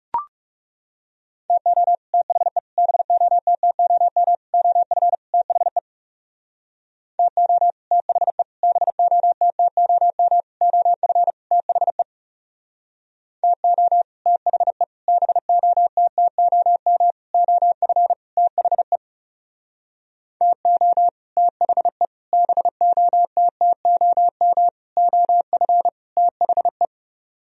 Morse code, heard as a keyed tone, about 38 wpm, 700 Hz. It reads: TO THE BOTTOM OF THE TO THE BOTTOM OF THE TO THE BOTTOM OF THE TO THE BOTTOM OF THE